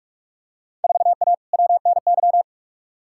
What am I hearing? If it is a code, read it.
Morse code, heard as a keyed tone, about 41 wpm, 700 Hz. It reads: 4A WNY
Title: Field Day Exchanges Encore - 45wpm